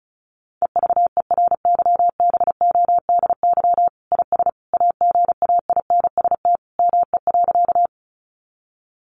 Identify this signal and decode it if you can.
Morse code, heard as a keyed tone, about 34 wpm, 700 Hz. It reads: EVERYBODY IS AGAINST ME.
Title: Sentences from Top 1400 Words 35wpm